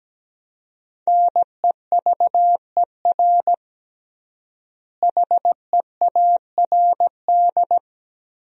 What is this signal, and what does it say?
Morse code, heard as a keyed tone, about 17 wpm, 700 Hz. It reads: NEVER HEARD
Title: 3x Word-Spacing - Sentences from Top 400 Words - 17wpm